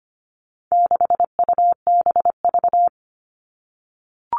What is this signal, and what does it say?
Morse code, heard as a keyed tone, about 25 wpm, 700 Hz. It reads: 6UBV
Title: All Cut Numbers - 4-Digit 25wpm